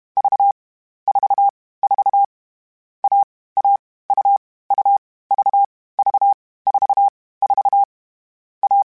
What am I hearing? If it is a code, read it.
Morse code, heard as a keyed tone, about 32 wpm, 800 Hz. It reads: V 4 4 A A U U V V 4 4 A
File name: Difficult Combos / Difficult Combos-mp3s / AUV AUV4 @30